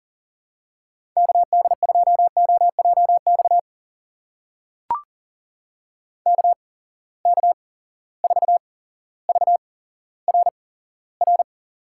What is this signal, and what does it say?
Morse code, heard as a keyed tone, about 39 wpm, spaced as for 17 wpm, 700 Hz. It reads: KD2OJX KK4VRR
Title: USA Call Signs - 6 Characters - 8x Character Spacing - 40wpm